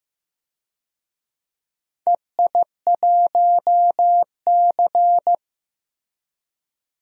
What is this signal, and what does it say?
Morse code, heard as a keyed tone, about 15 wpm, 700 Hz. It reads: EI1C
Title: DXCC-100 Call Signs - 4 Characters - Repeated 2x - 15wpm